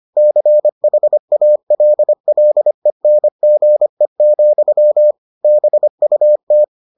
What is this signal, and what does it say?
Morse code, heard as a keyed tone, about 25 wpm, 600 Hz. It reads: CHALLENGE, BUT